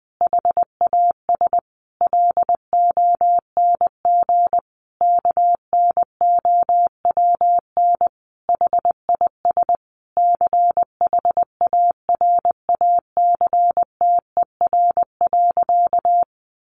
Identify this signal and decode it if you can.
Morse code, heard as a keyed tone, about 19 wpm, 700 Hz. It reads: HAS LONG KNOWN HIS CHARACTER.
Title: Sentences from Top 1200 Words 20wpm